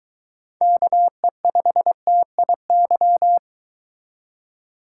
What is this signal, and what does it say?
Morse code, heard as a keyed tone, about 22 wpm, 700 Hz. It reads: KE5TIY